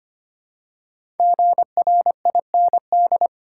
Morse code, heard as a keyed tone, about 25 wpm, 700 Hz. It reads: GRIND
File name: 5 Letter Words - First 3 Letters Given - 25wpm